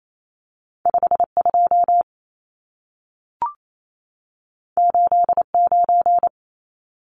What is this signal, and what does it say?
Morse code, heard as a keyed tone, about 28 wpm, 700 Hz. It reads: 52 89
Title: Numbers - 2-Digit 28wpm